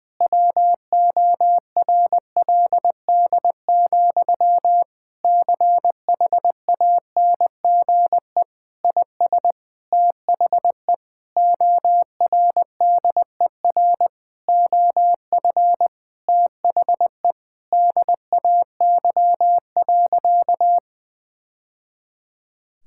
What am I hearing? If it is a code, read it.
Morse code, heard as a keyed tone, about 21 wpm, 700 Hz. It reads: WORLD, CHANGE IS THE ORDER OF THE DAY.
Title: Sentences from Top 1000 Words - Review Part 5 - 20wpm